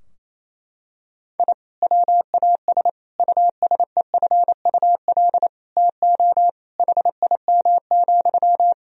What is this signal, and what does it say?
Morse code, heard as a keyed tone, about 28 wpm, 700 Hz. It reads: I WAS USEFUL TO HIM,
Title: Sentences from Top 1000 Words - Review Part 4 - 28wpm